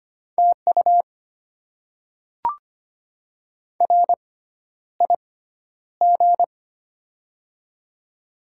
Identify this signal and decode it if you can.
Morse code, heard as a keyed tone, about 25 wpm, spaced as for 11 wpm, 700 Hz. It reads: TU RIG
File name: Top 20 QSO Elements - 6x Character Spacing - 25wpm